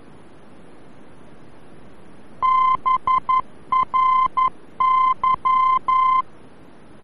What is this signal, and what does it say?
Morse code, heard as a keyed tone, about 11 wpm, 1040 Hz. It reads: BRY
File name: BRY-VOR-F